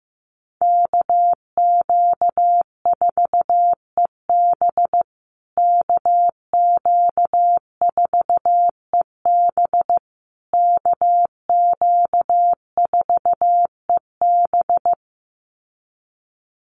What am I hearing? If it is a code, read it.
Morse code, heard as a keyed tone, about 15 wpm, 700 Hz. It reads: KQ4EB KQ4EB KQ4EB
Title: USA Call Signs Repeated 3x 15wpm